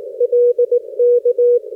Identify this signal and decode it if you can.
Morse code, heard as a keyed tone, about 18 wpm, 480 Hz. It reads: LK
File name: 60m qso